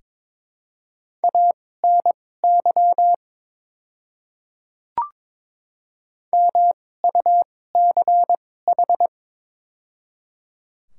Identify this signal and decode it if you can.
Morse code, heard as a keyed tone, about 22 wpm, spaced as for 16 wpm, 700 Hz. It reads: ANY MUCH
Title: 2x Character Spacing - Top 200 Words - 22wpm